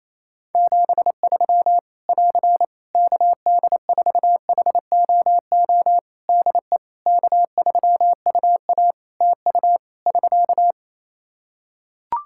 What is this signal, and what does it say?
Morse code, heard as a keyed tone, about 28 wpm, 700 Hz. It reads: 73 <AR> KD4HOO DE K3UA TU <SK>
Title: Practice QSO - Exchange 4 - Ending - 28wpm